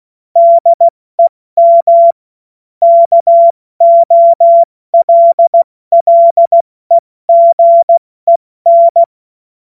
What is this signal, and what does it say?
Morse code, heard as a keyed tone, about 16 wpm, 680 Hz. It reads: DEM KOLLEGEN
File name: Morsekurs / Witze / Witz 28. März 2023 70 F80